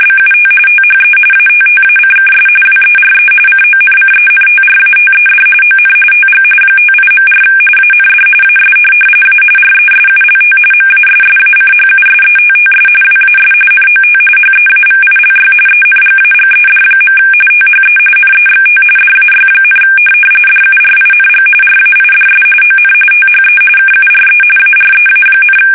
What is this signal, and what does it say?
RTTY, about 45.45 baud. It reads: BUT THE ATTACKS WERE CONCENTRATED ON TG 58.4 AND AS A RESULT MASSACHUSETTS WAS NOT HEAVILY ENGAGED. THE NEXT DAY, DURING RAIDS ON THE KURE AREA,
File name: fsk850 / USS-Massachusetts